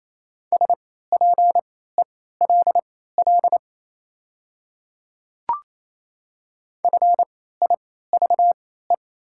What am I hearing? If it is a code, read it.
Morse code, heard as a keyed tone, about 27 wpm, spaced as for 16 wpm, 700 Hz. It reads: SPELL FIVE
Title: 3x Character Spacing - Top 500 Words - 28wpm